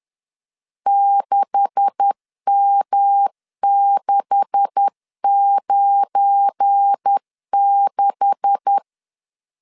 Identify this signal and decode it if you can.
Morse code, heard as a keyed tone, about 10 wpm, 800 Hz. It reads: 6M696
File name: LEZIONI / 9 286 10wpm 2735